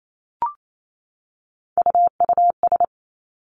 Morse code, heard as a keyed tone, about 28 wpm, 700 Hz. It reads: UUS
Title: Three Challenging Letter-Number Combinations 28wpm